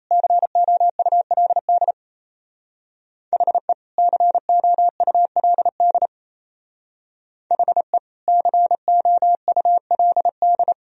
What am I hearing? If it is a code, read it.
Morse code, heard as a keyed tone, about 35 wpm, 700 Hz. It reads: COULD HE COULD HE COULD